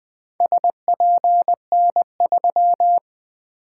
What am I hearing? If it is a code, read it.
Morse code, heard as a keyed tone, about 20 wpm, 700 Hz. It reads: SPN3
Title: Four Letter-Number Combinations - 20wpm